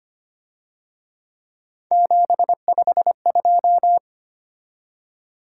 Morse code, heard as a keyed tone, about 28 wpm, 700 Hz. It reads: 752
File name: Speed-Racing - Three Letter-Number Combinations - 25wpm